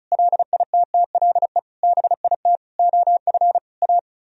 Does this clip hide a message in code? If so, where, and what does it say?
Yes: Morse code from 0.1 to 4.0 s, heard as a keyed tone, about 44 wpm, 700 Hz. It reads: LITTLE BIT OF A